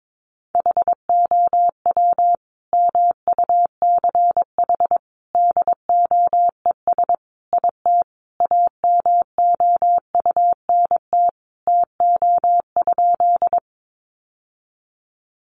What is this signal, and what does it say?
Morse code, heard as a keyed tone, about 22 wpm, 700 Hz. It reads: HOW MUCH DOES IT AMOUNT TO?
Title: Sentences from Top 700 Words 22wpm